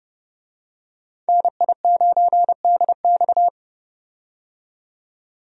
Morse code, heard as a keyed tone, about 32 wpm, 700 Hz. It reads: NI9DX
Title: USA Call Signs - 5 Characters - 30wpm